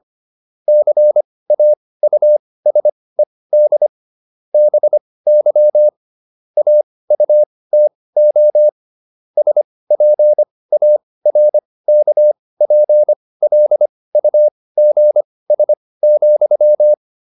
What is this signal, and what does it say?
Morse code, heard as a keyed tone, about 25 wpm, spaced as for 18 wpm, 600 Hz. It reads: CAUSED BY AUTO SPARKPLUGS,